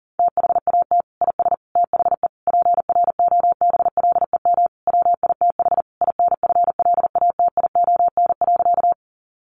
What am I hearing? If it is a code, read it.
Morse code, heard as a keyed tone, about 40 wpm, 700 Hz. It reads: THAT IS THE PROBLEM WITH INFLATION.
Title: Sentences from Top 1100 Words 40wpm